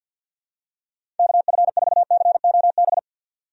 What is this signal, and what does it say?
Morse code, heard as a keyed tone, about 51 wpm, 700 Hz. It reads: KF4COB
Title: USA Call Signs - 6 Characters - 50wpm